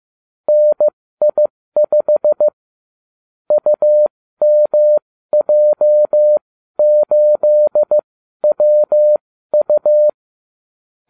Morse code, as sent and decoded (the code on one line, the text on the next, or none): -. .. ..... / ..- -- .--- ---.. .-- ..-
NI5 UMJ8WU